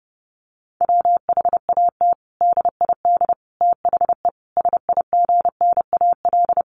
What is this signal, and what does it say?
Morse code, heard as a keyed tone, about 30 wpm, 700 Hz. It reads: WHAT DID THE SIGNAL
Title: Sentences from Top 1400 Words 30wpm